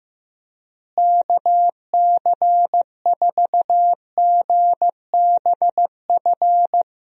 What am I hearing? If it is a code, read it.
Morse code, heard as a keyed tone, about 15 wpm, 700 Hz. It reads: KC4GBF
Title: USA Call Signs - 6 Characters - Repeated 3x - 15wpm